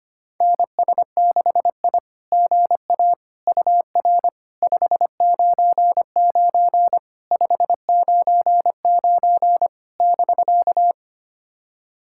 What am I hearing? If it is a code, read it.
Morse code, heard as a keyed tone, about 25 wpm, 660 Hz. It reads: NS6I GA UR 599 599 <BK>
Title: SOTA QSO Practice 25wpm